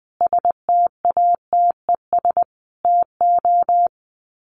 Morse code, heard as a keyed tone, about 20 wpm, 700 Hz. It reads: STATES TO
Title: Sets of 5 Words Encore 20wpm